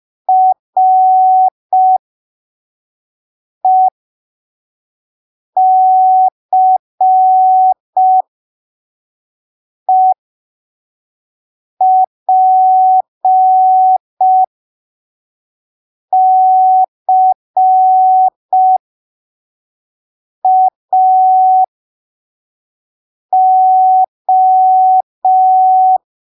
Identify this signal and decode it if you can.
Morse code, heard as a keyed tone, about 5 wpm, spaced as for 3 wpm, 750 Hz. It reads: RECEPCAO
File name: arquivos / texto / texto 01 5ppm